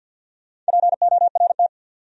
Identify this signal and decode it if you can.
Morse code, heard as a keyed tone, about 66 wpm, spaced as for 58 wpm, 700 Hz. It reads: PORT
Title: Speed-Racing Top 200 Words 50wpm